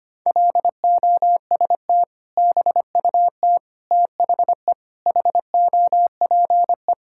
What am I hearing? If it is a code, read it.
Morse code, heard as a keyed tone, about 25 wpm, 700 Hz. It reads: LOST BUT THE HOPE
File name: practice / 25wpm / Sentences with 2-Word Phrases - 25wpm